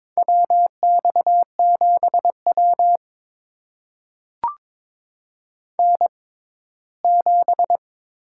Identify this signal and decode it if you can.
Morse code, heard as a keyed tone, about 23 wpm, spaced as for 15 wpm, 700 Hz. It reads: WX7W N7
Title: USA Call Signs - 4 Characters - 6x Character Spacing - 22wpm